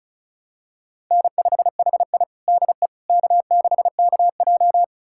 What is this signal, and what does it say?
Morse code, heard as a keyed tone, about 36 wpm, 700 Hz. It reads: N5HI DE K6KJ
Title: USA Callsign DE Callsign - 4 Characters - 35wpm